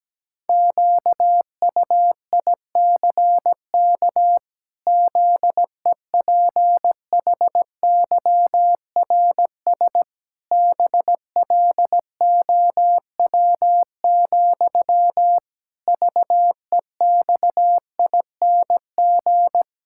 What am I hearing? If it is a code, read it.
Morse code, heard as a keyed tone, about 17 wpm, 700 Hz. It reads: QUICK ZEPHYRS BLOW, VEXING
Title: Pangram Challenge v2 - 17wpm